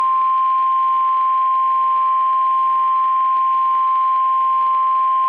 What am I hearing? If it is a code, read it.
Emergency Alert System (SAME) tones (the 1050 Hz attention tone). It is heard from 0.8 to 4.9 s.
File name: eas messages / EAS WXM31 Elmira